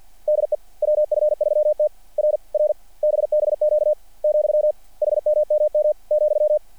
Morse code, heard as a keyed tone, about 48 wpm, 600 Hz. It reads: DE WF3T R R BBQ , HMMM ,
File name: archived / CRSnet